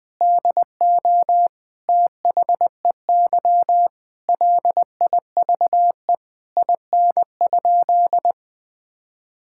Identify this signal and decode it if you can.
Morse code, heard as a keyed tone, about 20 wpm, 700 Hz. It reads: DO THEY LIVE IN?
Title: Sentences from Top 1000 Words - Review Part 5 - 20wpm